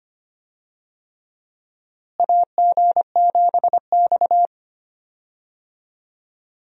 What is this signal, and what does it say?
Morse code, heard as a keyed tone, about 25 wpm, 700 Hz. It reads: AG7X
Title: USA Call Signs - 4 Characters - Repeated 3x - 25wpm